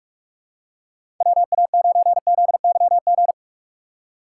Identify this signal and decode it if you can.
Morse code, heard as a keyed tone, about 44 wpm, 700 Hz. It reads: WA9ZYG
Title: USA Call Signs - 6 Characters - 6x Character Spacing - 45wpm